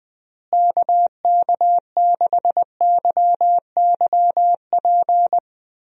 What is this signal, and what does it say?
Morse code, heard as a keyed tone, about 20 wpm, 700 Hz. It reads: KK6YYP
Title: USA Call Signs - 6 Characters - Repeated 2x - 20wpm